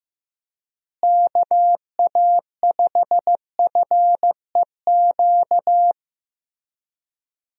Morse code, heard as a keyed tone, about 15 wpm, 700 Hz. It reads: KA5FEQ
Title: USA Call Signs - 6 Characters - Repeated 3x - 15wpm